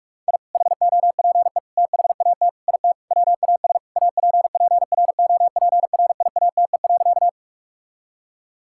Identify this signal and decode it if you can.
Morse code, heard as a keyed tone, about 46 wpm, 700 Hz. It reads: I HOPE THAT IT WAS APPROPRIATE.
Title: Sentences from Top 1000 Words - Review Part 5 - 45wpm